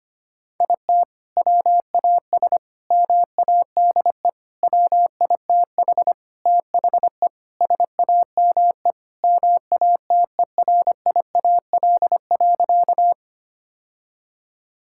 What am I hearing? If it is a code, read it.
Morse code, heard as a keyed tone, about 25 wpm, 700 Hz. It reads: IT WAS MADE WITH THE SAME MATERIAL.
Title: Sentences from Top 1000 Words - Review Part 5 - 25wpm